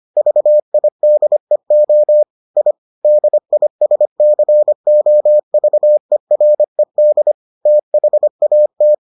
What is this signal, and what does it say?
Morse code, heard as a keyed tone, about 25 wpm, 600 Hz. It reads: VIDEO I DISCOVERED THAT